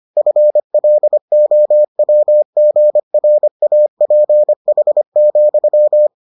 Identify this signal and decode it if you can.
Morse code, heard as a keyed tone, about 25 wpm, 600 Hz. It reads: FLOWGRAPH,